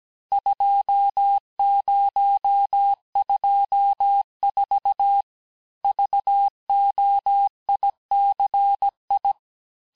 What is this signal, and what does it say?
Morse code, heard as a keyed tone, about 17 wpm, 800 Hz. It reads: 2024 VOICI
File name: cours-du-lundi files / corrige-cours-1581-texte-chiffre-vitesse-16-mpm